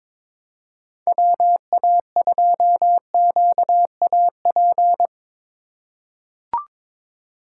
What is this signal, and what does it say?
Morse code, heard as a keyed tone, about 23 wpm, 700 Hz. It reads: WA2QAP
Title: USA Call Signs - 6 Characters - 3x Character Spacing - 22wpm